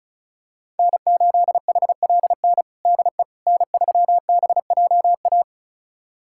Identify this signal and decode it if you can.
Morse code, heard as a keyed tone, about 34 wpm, 700 Hz. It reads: N8HLN DE N3BJA
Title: USA Callsign DE Callsign - 5 Characters - 35wpm